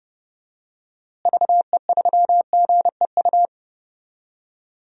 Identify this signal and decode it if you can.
Morse code, heard as a keyed tone, about 30 wpm, 700 Hz. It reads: VE3GEU
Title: DXCC-100 Call Signs - 6 Characters - Repeated 3x - 30wpm